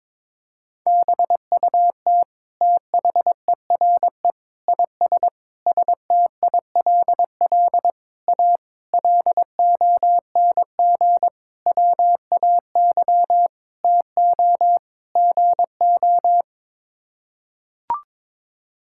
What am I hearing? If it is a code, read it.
Morse code, heard as a keyed tone, about 22 wpm, 700 Hz. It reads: BUT THERE IS STILL A LONG WAY TO GO
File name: Sets of 9 Words - 22wpm